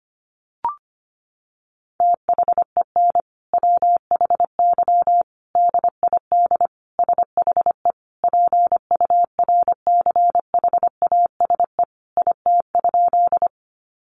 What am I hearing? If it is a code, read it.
Morse code, heard as a keyed tone, about 25 wpm, 700 Hz. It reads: THEN WHY DID SHE PURCHASE IT?
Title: Sentences from Top 800 Words 25wpm